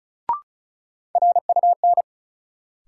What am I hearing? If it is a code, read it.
Morse code, heard as a keyed tone, about 35 wpm, 700 Hz. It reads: RUN
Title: Sentences from Top 300 Words - New Word Drill - 35wpm